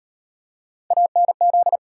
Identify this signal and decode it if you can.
Morse code, heard as a keyed tone, about 37 wpm, 700 Hz. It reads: ANZ